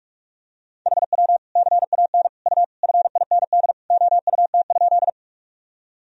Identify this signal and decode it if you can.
Morse code, heard as a keyed tone, about 41 wpm, 700 Hz. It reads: HW CAN U FIND OUT?
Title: Sentences from Top 200 Words and Abbreviations 45wpm